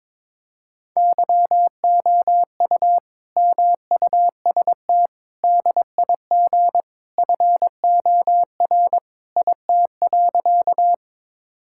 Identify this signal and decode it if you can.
Morse code, heard as a keyed tone, about 22 wpm, 700 Hz. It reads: YOU MUST DIG FOR IT.
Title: Sentences from Top 1600 Words 22wpm